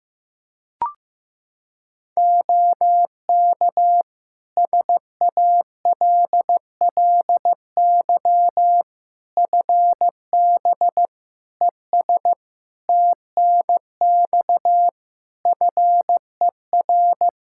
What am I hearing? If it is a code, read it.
Morse code, heard as a keyed tone, about 16 wpm, 700 Hz. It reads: OK SALLY FB ES TNX FER
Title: Practice QSO - Exchange 2 - Info - 15wpm